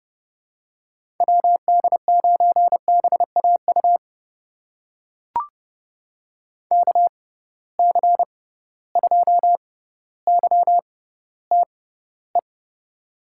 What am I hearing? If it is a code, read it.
Morse code, heard as a keyed tone, about 30 wpm, spaced as for 15 wpm, 700 Hz. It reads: WD9BAU KC2YTE
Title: USA Call Signs - 6 Characters - 6x Character Spacing - 30wpm